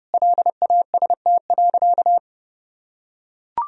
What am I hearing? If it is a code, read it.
Morse code, heard as a keyed tone, about 30 wpm, 700 Hz. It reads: LAST.
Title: Sentences from Top 1000 Words - Review Part 2 - 30wpm